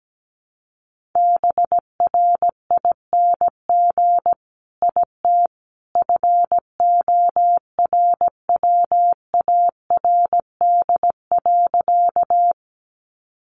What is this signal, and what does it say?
Morse code, heard as a keyed tone, about 17 wpm, 700 Hz. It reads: BRING IT FORWARD.
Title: Sentences from Top 1300 Words 17wpm